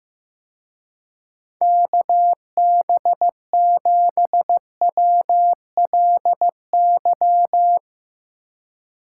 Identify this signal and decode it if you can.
Morse code, heard as a keyed tone, about 16 wpm, 700 Hz. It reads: KB7WLY
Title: Speed Racing - USA 2x3 Call Signs 15wpm